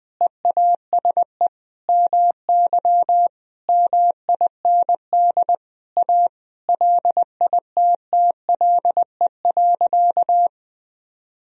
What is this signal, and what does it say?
Morse code, heard as a keyed tone, about 21 wpm, 700 Hz. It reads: EASE MY MIND A LITTLE.
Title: Sentences from Top 1000 Words - Review Part 1 - 20wpm